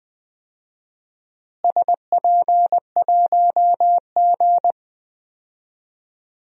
Morse code, heard as a keyed tone, about 20 wpm, 700 Hz. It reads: SP1G
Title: DXCC-100 Call Signs - 4 Characters - Repeated 2x - 20wpm